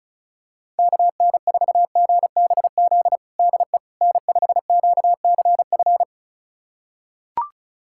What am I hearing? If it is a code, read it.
Morse code, heard as a keyed tone, about 36 wpm, 700 Hz. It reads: KN4GBZ DE N5QCF
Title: USA Callsign DE Callsign 35wpm